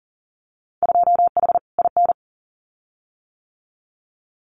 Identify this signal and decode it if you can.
Morse code, heard as a keyed tone, about 38 wpm, 700 Hz. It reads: 2H IN
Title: Winter Field Day Exchanges Encore 40wpm